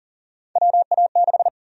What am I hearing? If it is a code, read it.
Morse code, heard as a keyed tone, about 40 wpm, 700 Hz. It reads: WA6
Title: USA Call Sign Prefixes - Repeated 2x - 40wpm